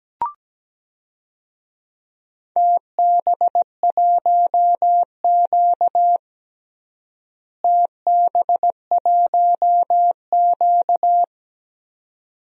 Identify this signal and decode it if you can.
Morse code, heard as a keyed tone, about 17 wpm, 700 Hz. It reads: TB1Q TB1Q
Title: DXCC-100 Call Signs - 4 Characters - Repeated 2x - 17wpm